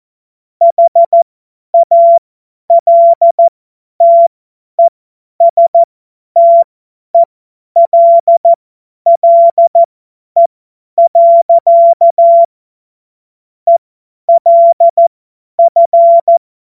Morse code, heard as a keyed tone, about 14 wpm, spaced as for 10 wpm, 680 Hz. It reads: HALTESTELLE. ELF
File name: Morsekurs / Witze / Witz 26. April 2022 50 F70